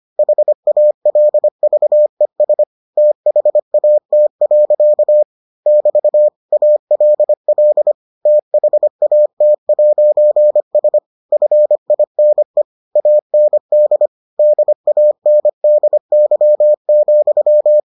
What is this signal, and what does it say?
Morse code, heard as a keyed tone, about 25 wpm, 600 Hz. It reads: HALVES THAT. <BT> ALL THAT'S FINE AND DANDY,